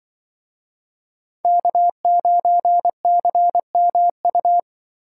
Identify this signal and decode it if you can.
Morse code, heard as a keyed tone, about 24 wpm, 700 Hz. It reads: K9CMU
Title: Speed Racing - USA Call Signs 20wpm